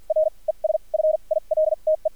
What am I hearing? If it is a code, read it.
Morse code, heard as a keyed tone, about 50 wpm, spaced as for 38 wpm, 630 Hz. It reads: R ES FILTE
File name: archived / CRSnet